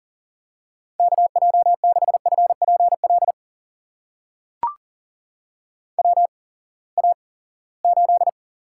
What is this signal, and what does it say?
Morse code, heard as a keyed tone, about 37 wpm, spaced as for 22 wpm, 700 Hz. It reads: KJ6FPL WA8
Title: USA Call Signs - 6 Characters - 8x Character Spacing - 40wpm